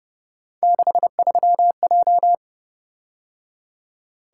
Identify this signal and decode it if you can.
Morse code, heard as a keyed tone, about 29 wpm, 700 Hz. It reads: 63J